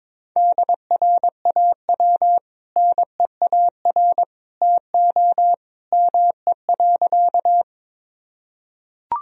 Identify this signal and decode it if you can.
Morse code, heard as a keyed tone, about 22 wpm, 700 Hz. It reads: DRAW NEAR TO ME.
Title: Sentences from Top 1000 Words - Review Part 1 - 22wpm